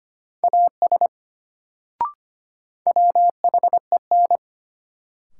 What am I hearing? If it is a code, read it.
Morse code, heard as a keyed tone, about 25 wpm, 700 Hz. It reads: AS WHEN
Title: Sentences from Top 100 Words - New Word Drill - 25wpm